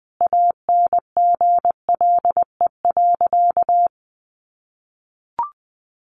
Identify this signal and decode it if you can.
Morse code, heard as a keyed tone, about 19 wpm, 700 Hz. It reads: ANGLE.
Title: Sentences from Top 1500 Words 20wpm